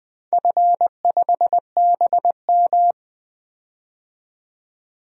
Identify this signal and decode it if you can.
Morse code, heard as a keyed tone, about 20 wpm, 700 Hz. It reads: F5BM
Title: DXCC-100 Call Signs - 4 Characters - 20wpm